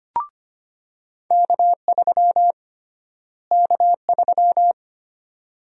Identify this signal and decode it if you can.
Morse code, heard as a keyed tone, about 25 wpm, 700 Hz. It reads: K3 K3
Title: USA Call Sign Prefixes - Repeated 2x - 25wpm